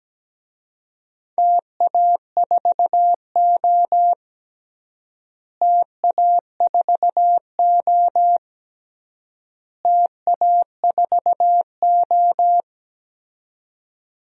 Morse code, heard as a keyed tone, about 17 wpm, 700 Hz. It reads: TA4O TA4O TA4O
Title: DXCC-100 Call Signs - 4 Characters - Repeated 3x - 17wpm